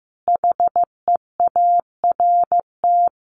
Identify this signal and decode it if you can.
Morse code, heard as a keyed tone, about 15 wpm, 700 Hz. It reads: HEART
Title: Sentences from Top 1500 Words 15wpm